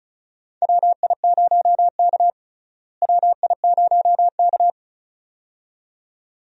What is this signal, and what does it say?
Morse code, heard as a keyed tone, about 36 wpm, 700 Hz. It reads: WI0K WI0K
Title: USA Call Signs - 4 Characters - Repeated 2x - 35wpm